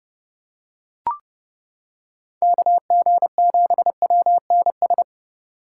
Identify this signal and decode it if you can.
Morse code, heard as a keyed tone, about 29 wpm, 700 Hz. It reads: KG7WNS